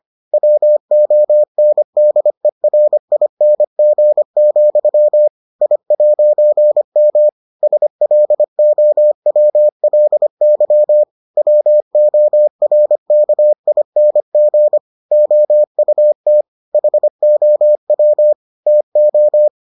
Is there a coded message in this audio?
Yes: Morse code from 0.3 to 19.5 s, heard as a keyed tone, about 25 wpm, 600 Hz. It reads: WONDERING, I'M SLOWLY WORKING OUT HOW TO